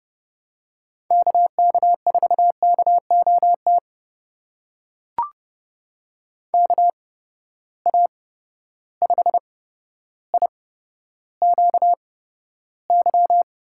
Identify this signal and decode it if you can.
Morse code, heard as a keyed tone, about 31 wpm, spaced as for 13 wpm, 700 Hz. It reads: KK4KOT KA5IQY
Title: USA Call Signs - 6 Characters - 8x Character Spacing - 30wpm